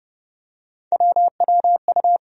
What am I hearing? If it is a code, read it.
Morse code, heard as a keyed tone, about 30 wpm, 700 Hz. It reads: WWU
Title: USA Call Sign Postfixes - 3 Letters 30wpm